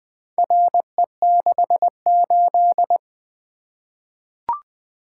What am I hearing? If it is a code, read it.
Morse code, heard as a keyed tone, about 20 wpm, 700 Hz. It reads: RE68
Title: Four Letter-Number Combinations - 20wpm